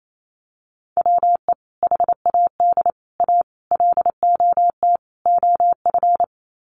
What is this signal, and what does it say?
Morse code, heard as a keyed tone, about 28 wpm, 690 Hz. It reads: WE HAD A LOT OF
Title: Sets of 5 Words Encore 28wpm